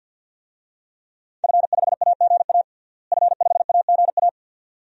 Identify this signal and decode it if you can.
Morse code, heard as a keyed tone, about 53 wpm, 700 Hz. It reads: F5AGA F5AGA
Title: DXCC-100 Call Signs - 5 Characters - Repeated 2x - 50wpm